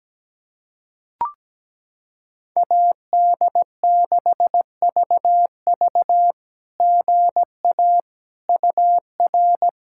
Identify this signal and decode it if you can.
Morse code, heard as a keyed tone, about 17 wpm, 700 Hz. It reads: AD6VV GA UR
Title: SOTA QSO Practice 17wpm